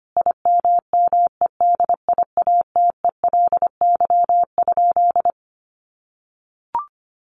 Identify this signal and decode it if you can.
Morse code, heard as a keyed tone, about 25 wpm, 700 Hz. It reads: IMMEDIATELY?
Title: Sentences from Top 1300 Words 25wpm